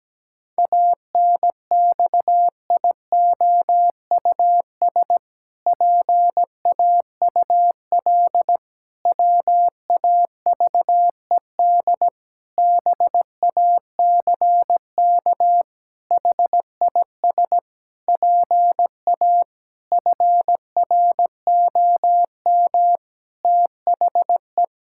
Morse code, heard as a keyed tone, about 17 wpm, 700 Hz. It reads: ANXIOUS PAUL WAVED BACK HIS PA FROM THE
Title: Pangram Challenge v2 - 17wpm